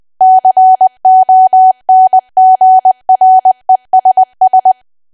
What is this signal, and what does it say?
Morse code, heard as a keyed tone, about 20 wpm, 750 Hz. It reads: CONGRESS
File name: cwa / W1AW-Level3 / 20WPM